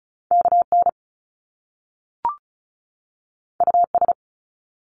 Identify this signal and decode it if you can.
Morse code, heard as a keyed tone, about 36 wpm, 700 Hz. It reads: KN US
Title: Top Two Letter Combinations in Words 35wpm